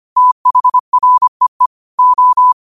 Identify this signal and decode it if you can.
Morse code, heard as a keyed tone, about 25 wpm, 1000 Hz. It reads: THREE O